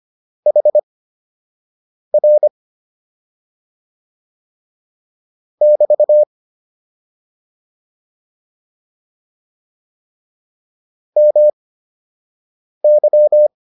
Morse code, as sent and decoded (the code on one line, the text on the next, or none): .... .-. / -...- / -- -.--
HR <BT> MY